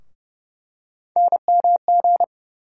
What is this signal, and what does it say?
Morse code, heard as a keyed tone, about 30 wpm, 700 Hz. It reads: NMG
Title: USA Call Sign Postfixes - 3 Letters 30wpm